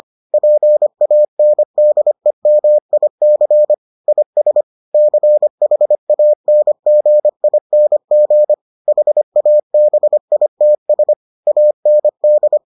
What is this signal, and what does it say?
Morse code, heard as a keyed tone, about 25 wpm, 600 Hz. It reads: PANDEMIC IS CHANGING HABITS AND